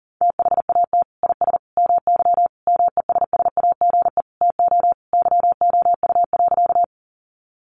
Morse code, heard as a keyed tone, about 40 wpm, 700 Hz. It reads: THAT IS MY MESSAGE TO YOU.
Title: Sentences from Top 1100 Words 40wpm